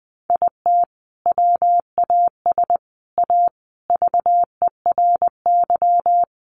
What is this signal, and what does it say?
Morse code, heard as a keyed tone, about 20 wpm, 700 Hz. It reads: IT WAS A VERY
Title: Sentences from Top 1400 Words 20wpm